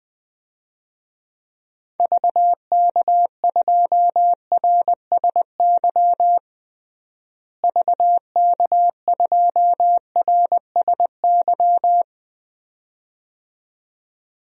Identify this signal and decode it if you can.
Morse code, heard as a keyed tone, about 20 wpm, 700 Hz. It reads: VK2RSY VK2RSY
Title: DXCC-100 Call Signs - 6 Characters - Repeated 2x - 20wpm